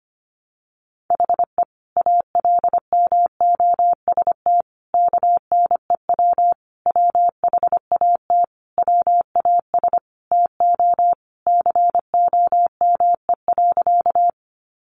Morse code, heard as a keyed tone, about 25 wpm, 700 Hz. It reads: HE ALMOST KNEW WHAT WAS TO COME.
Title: Sentences from Top 700 Words 25wpm